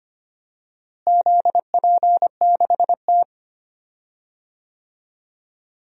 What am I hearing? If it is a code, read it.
Morse code, heard as a keyed tone, about 25 wpm, 700 Hz. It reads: ZP6T
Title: DXCC-100 Call Signs - 4 Characters - Repeated 2x - 25wpm